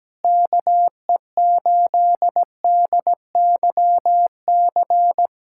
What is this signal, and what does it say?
Morse code, heard as a keyed tone, about 17 wpm, 700 Hz. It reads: KE8DYC
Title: USA Callsign DE Callsign - 6 Characters - 17wpm